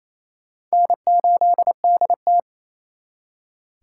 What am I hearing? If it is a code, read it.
Morse code, heard as a keyed tone, about 35 wpm, 700 Hz. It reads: N8DT
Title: Speed Racing - USA 1x2 Call Signs 28wpm